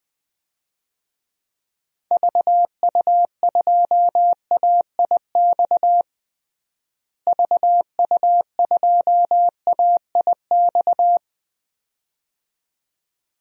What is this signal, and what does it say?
Morse code, heard as a keyed tone, about 20 wpm, 700 Hz. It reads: VU2AIX VU2AIX
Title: DXCC-100 Call Signs - 6 Characters - Repeated 2x - 20wpm